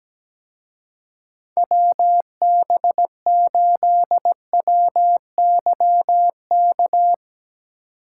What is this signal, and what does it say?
Morse code, heard as a keyed tone, about 18 wpm, 700 Hz. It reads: WB8WYK
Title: Speed Racing - USA 2x3 Call Signs 17wpm